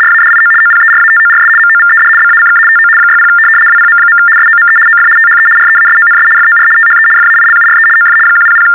RTTY, about 75 baud. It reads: RADIO OPERATORS MOVED BACK TO 60 WPM. SOME OF THE REASONS FOR THE FAILURE OF 100
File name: teletype-RTTY-75